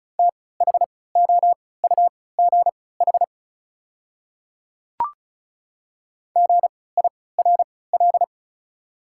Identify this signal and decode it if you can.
Morse code, heard as a keyed tone, about 36 wpm, spaced as for 20 wpm, 690 Hz. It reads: THOUGH GIRL
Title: 3x Character Spacing - Top 500 Words - 35wpm